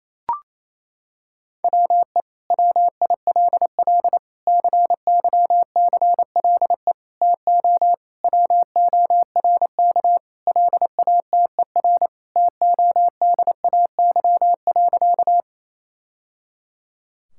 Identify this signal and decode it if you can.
Morse code, heard as a keyed tone, about 28 wpm, 700 Hz. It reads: WE WILL CYCLE TO WORK LATER TODAY.
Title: Sentences from Top 1000 Words - Review Part 6 - 28wpm